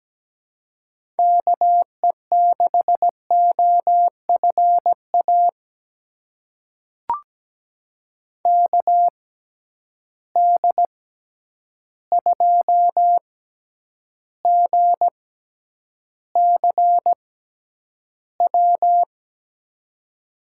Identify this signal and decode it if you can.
Morse code, heard as a keyed tone, about 17 wpm, spaced as for 9 wpm, 700 Hz. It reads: KE6OFA KD2GCW
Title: USA Call Signs - 6 Characters - 6x Character Spacing - 17wpm